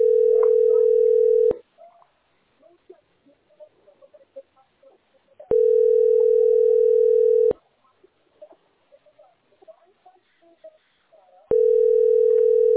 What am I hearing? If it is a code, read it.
Ringback tone (North America), 440 + 480 Hz.